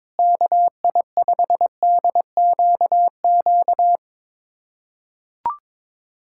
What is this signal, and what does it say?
Morse code, heard as a keyed tone, about 22 wpm, 700 Hz. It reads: KI5DQQ
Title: USA Callsign DE Callsign - 6 Characters - 22wpm